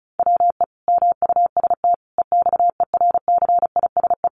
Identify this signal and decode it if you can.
Morse code, heard as a keyed tone, about 34 wpm, 700 Hz. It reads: WE MUST EXERCISE
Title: Sentences from Top 1200 Words 35wpm